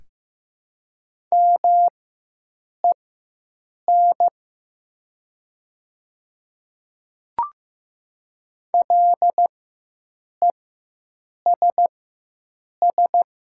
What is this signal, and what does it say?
Morse code, heard as a keyed tone, about 15 wpm, spaced as for 7 wpm, 700 Hz. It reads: MEN LESS
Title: 4x Character Spacing - Top 500 Words - 15wpm